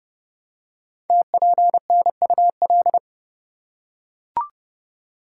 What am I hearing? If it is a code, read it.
Morse code, heard as a keyed tone, about 29 wpm, 700 Hz. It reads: TPNUL
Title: Code Groups - 5 Letters - 30wpm